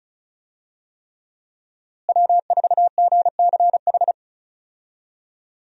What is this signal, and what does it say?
Morse code, heard as a keyed tone, about 36 wpm, 700 Hz. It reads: W4GCH
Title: USA Call Signs - 5 Characters - Repeated 3x - 35wpm